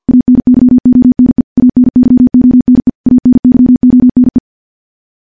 Morse code, heard as a keyed tone, about 50 wpm, 250 Hz. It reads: ANYONE ANYONE ANYONE
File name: morse / morse-training-50WPM